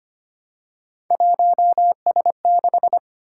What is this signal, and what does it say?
Morse code, heard as a keyed tone, about 25 wpm, 700 Hz. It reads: 1S6
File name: Three Letter-Number Combinations - 25wpm